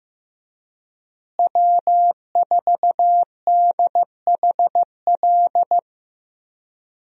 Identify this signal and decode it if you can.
Morse code, heard as a keyed tone, about 15 wpm, 700 Hz. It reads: W4DHL
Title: USA Call Signs - 5 Characters - Repeated 2x - 15wpm